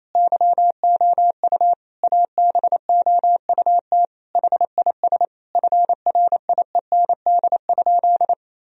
Morse code, heard as a keyed tone, about 28 wpm, 700 Hz. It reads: YOU ABOUT HIS FRIEND?
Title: Sentences from Top 1000 Words - Review Part 3 - 28wpm